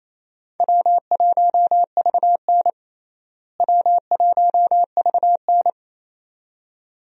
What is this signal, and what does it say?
Morse code, heard as a keyed tone, about 28 wpm, 700 Hz. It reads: W1VN W1VN
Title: USA Call Signs - 4 Characters - Repeated 2x - 28wpm